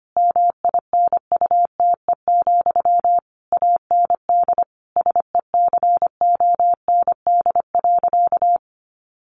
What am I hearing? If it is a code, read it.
Morse code, heard as a keyed tone, about 25 wpm, 700 Hz. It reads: MINUTE, AND SECOND.
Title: A Sound Of Thunder 25wpm